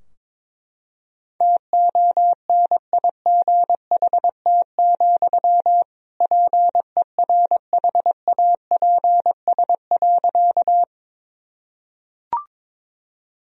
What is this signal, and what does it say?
Morse code, heard as a keyed tone, about 22 wpm, 700 Hz. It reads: TONIGHT, PERHAPS.
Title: Sentences from Top 1000 Words - Review Part 3 - 22wpm